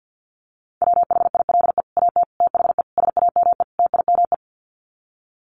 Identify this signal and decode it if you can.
Morse code, heard as a keyed tone, about 50 wpm, 700 Hz. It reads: WHILE AT THE SAME TIME
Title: Sets of 5 Words 50wpm